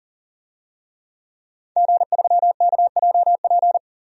Morse code, heard as a keyed tone, about 40 wpm, 700 Hz. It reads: G3KJP
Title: DXCC-100 Call Signs - 5 Characters - 40wpm